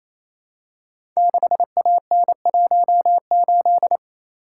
Morse code, heard as a keyed tone, about 27 wpm, 700 Hz. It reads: 6AN18
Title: Code Groups - 5 Letter:Numbers - 28wpm